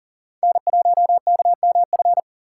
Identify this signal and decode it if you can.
Morse code, heard as a keyed tone, about 40 wpm, 700 Hz. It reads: N1KMF
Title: USA Call Signs - 2x Character Spacing - 40wpm